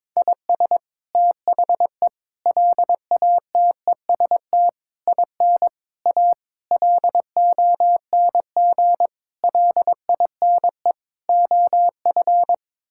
Morse code, heard as a keyed tone, about 22 wpm, 700 Hz. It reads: IS THE LATEST IN A LONG LINE OF
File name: practice / 22wpm / Sets of 8 Words - 22wpm